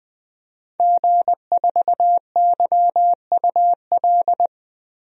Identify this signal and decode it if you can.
Morse code, heard as a keyed tone, about 20 wpm, 700 Hz. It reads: G4YUL
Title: DXCC-100 Call Signs - 5 Characters - Repeated 3x - 20wpm